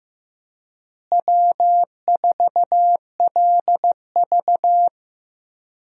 Morse code, heard as a keyed tone, about 16 wpm, 700 Hz. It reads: W4LV
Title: USA Call Signs - 4 Characters - Repeated 2x - 15wpm